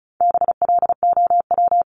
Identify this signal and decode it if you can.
Morse code, heard as a keyed tone, about 34 wpm, 700 Hz. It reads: BLOW
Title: Top 1751-2000 Words 35wpm